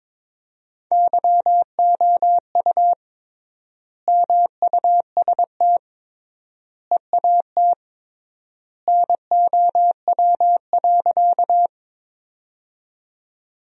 Morse code, heard as a keyed tone, about 21 wpm, 700 Hz. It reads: YOU MUST EAT NOW.
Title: 3x Word-Spacing - Sentences from Top 300 Words - 22wpm